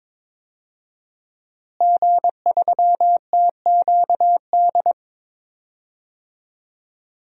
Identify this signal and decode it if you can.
Morse code, heard as a keyed tone, about 22 wpm, 700 Hz. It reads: G3TQD
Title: DXCC-100 Call Signs - 5 Characters - Repeated 3x - 22wpm